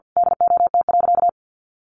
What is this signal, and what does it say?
Morse code, heard as a keyed tone, about 50 wpm, 700 Hz. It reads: DOT.
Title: Sentences from Top 1500 Words 50wpm